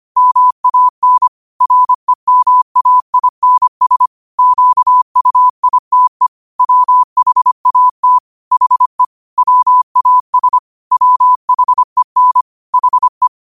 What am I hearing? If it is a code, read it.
Morse code, heard as a keyed tone, about 25 wpm, 1000 Hz. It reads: MAN REMAINS QUITE WHAT HE WAS WHEN HE
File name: qotdfiles / 25wpm